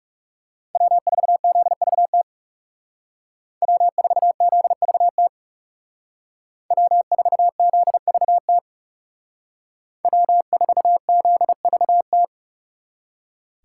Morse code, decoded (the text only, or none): W4ZVT W4ZVT W4ZVT W4ZVT